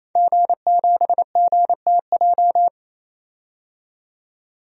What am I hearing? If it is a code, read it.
Morse code, heard as a keyed tone, about 28 wpm, 700 Hz. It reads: G7GTJ
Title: DXCC-100 Call Signs - 5 Characters - 28wpm